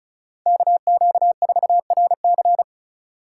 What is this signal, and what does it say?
Morse code, heard as a keyed tone, about 34 wpm, 700 Hz. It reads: KQ4RC
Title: USA Call Signs - 5 Characters - 4x Character Spacing - 35wpm